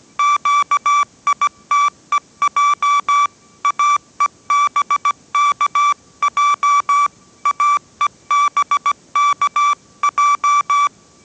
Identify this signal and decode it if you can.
Morse code, heard as a keyed tone, about 17 wpm, 1160 Hz. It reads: QITEJ AEBKJ AEBKJ